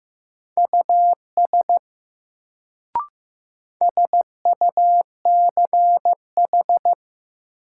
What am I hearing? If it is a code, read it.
Morse code, heard as a keyed tone, about 16 wpm, 700 Hz. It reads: US SUCH
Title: Sets of 2 Words - New Word Drill - 15wpm